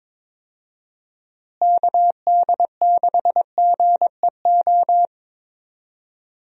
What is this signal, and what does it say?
Morse code, heard as a keyed tone, about 22 wpm, 700 Hz. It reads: KD6GEO
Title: USA Call Signs - 6 Characters - Repeated 2x - 22wpm